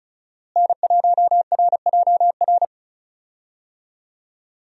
Morse code, heard as a keyed tone, about 36 wpm, 700 Hz. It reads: N1RJR
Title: USA Call Signs - 5 Characters - Repeated 3x - 35wpm